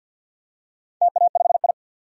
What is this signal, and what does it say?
Morse code, heard as a keyed tone, about 51 wpm, 700 Hz. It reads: TA5I
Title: DXCC-100 Call Signs - 4 Characters - 50wpm